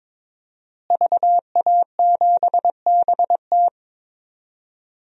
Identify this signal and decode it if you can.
Morse code, heard as a keyed tone, about 22 wpm, 700 Hz. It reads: VA7BT
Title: DXCC-100 Call Signs - 5 Characters - Repeated 2x - 22wpm